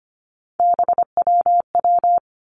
Morse code, heard as a keyed tone, about 25 wpm, 700 Hz. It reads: BWW
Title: Three Challenging Letter-Number Combinations 25wpm